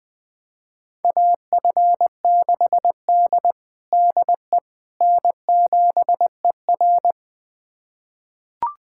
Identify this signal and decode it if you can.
Morse code, heard as a keyed tone, about 20 wpm, 700 Hz. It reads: AF6D DE N7ER
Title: USA Callsign DE Callsign - 4 Characters - 20wpm